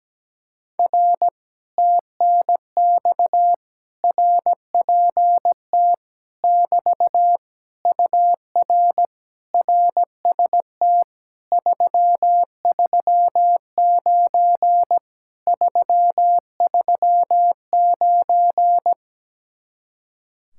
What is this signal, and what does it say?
Morse code, heard as a keyed tone, about 17 wpm, 700 Hz. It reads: R TNX RPT <BT> UR RST 339 339
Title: Speed-Racing Common QSO Phrases Encore - 17wpm